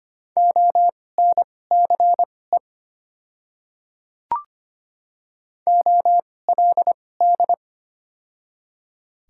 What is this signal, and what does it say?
Morse code, heard as a keyed tone, about 25 wpm, spaced as for 18 wpm, 700 Hz. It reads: ONCE OLD
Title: 2x Character Spacing - Top 300 Words - 25wpm